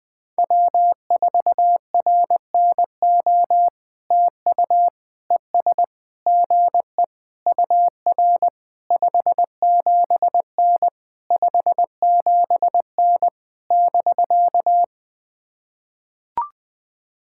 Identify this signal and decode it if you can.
Morse code, heard as a keyed tone, about 20 wpm, 700 Hz. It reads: W4RNO TU ES GE UR 57N 57N <BK>
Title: POTA QSO Practice - 20wpm